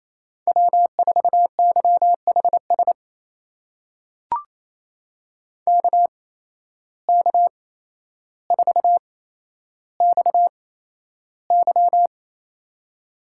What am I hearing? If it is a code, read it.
Morse code, heard as a keyed tone, about 28 wpm, spaced as for 12 wpm, 700 Hz. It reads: W4YHS KK4XY
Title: USA Call Signs - 8x Character Spacing - 28wpm